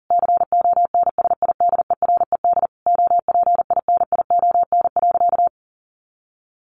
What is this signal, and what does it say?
Morse code, heard as a keyed tone, about 40 wpm, 700 Hz. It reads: CONSIDERED OPINION.
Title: A Sound Of Thunder 40wpm